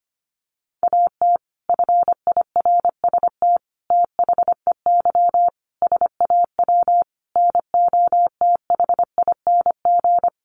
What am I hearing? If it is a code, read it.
Morse code, heard as a keyed tone, about 25 wpm, 700 Hz. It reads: AT FIRST THEY SAW NOTHING
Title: Sentences from Top 600 Words 25wpm